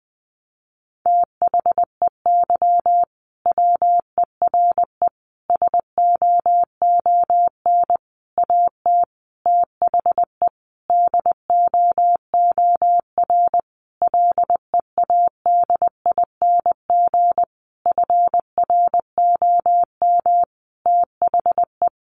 Morse code, heard as a keyed tone, about 20 wpm, 700 Hz. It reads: THEY WERE SOON AT THE DOOR LEADING FROM THE
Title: Sentences from Top 1200 Words 20wpm